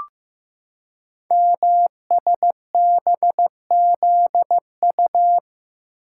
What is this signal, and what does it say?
Morse code, heard as a keyed tone, about 15 wpm, 700 Hz. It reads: MSBZU
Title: Code Groups - 5 Letters - 15wpm